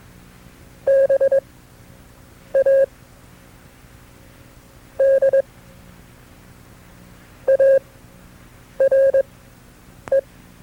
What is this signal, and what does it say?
Morse code, heard as a keyed tone, about 21 wpm, spaced as for 6 wpm, 540 Hz. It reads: BAD ARE
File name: FreeFiles / CW-practice / KB6MT-tape2-side2